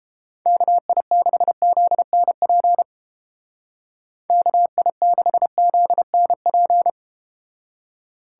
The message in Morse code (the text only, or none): KI6ZNP KI6ZNP